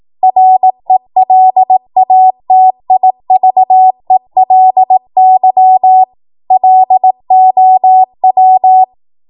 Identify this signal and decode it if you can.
Morse code, heard as a keyed tone, about 18 wpm, 750 Hz. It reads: RELATIVELY LOW
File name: Morse / 18WPM